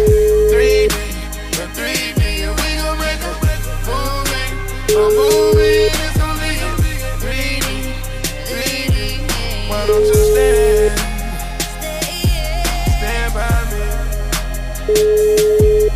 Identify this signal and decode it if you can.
Ringback tone (Europe, 425 Hz), with noise.